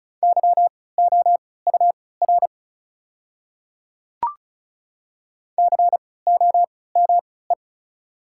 Morse code, heard as a keyed tone, about 36 wpm, spaced as for 20 wpm, 700 Hz. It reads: YOUR COME
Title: 3x Character Spacing - Top 100 Words - 35wpm